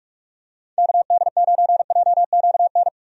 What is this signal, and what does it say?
Morse code, heard as a keyed tone, about 46 wpm, 700 Hz. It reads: KD9JQN
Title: USA Call Signs - 6 Characters - 45wpm